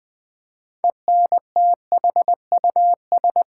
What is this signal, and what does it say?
Morse code, heard as a keyed tone, about 20 wpm, 700 Hz. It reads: ENTHUS
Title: Build A Word - 6 Letters At A Time - 12-Letter Words - 20wpm